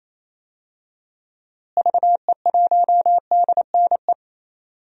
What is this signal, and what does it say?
Morse code, heard as a keyed tone, about 28 wpm, 700 Hz. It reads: VE1DNE
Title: DXCC-100 Call Signs - 6 Characters - Repeated 2x - 28wpm